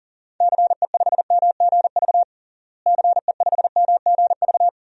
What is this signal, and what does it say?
Morse code, heard as a keyed tone, about 40 wpm, 700 Hz. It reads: CE5MGV CE5MGV
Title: DXCC-100 Call Signs - 6 Characters - Repeated 3x - 40wpm